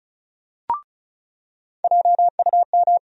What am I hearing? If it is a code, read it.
Morse code, heard as a keyed tone, about 34 wpm, 700 Hz. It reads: JUM
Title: USA Call Sign Postfixes - 3 Letters 35wpm